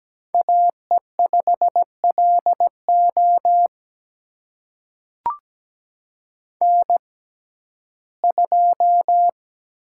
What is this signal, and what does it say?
Morse code, heard as a keyed tone, about 17 wpm, spaced as for 12 wpm, 700 Hz. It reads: AE5LO N2
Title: USA Call Signs - 5 Characters - 6x Character Spacing - 17wpm